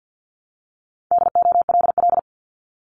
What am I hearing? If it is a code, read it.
Morse code, heard as a keyed tone, about 50 wpm, 700 Hz. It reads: DOLL
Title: Multi-letter Phonemes Repeated 3x 50wpm